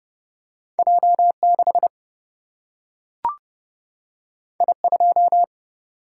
Morse code, heard as a keyed tone, about 30 wpm, 700 Hz. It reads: J6 I2
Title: Single-Letter Single-Number 30wpm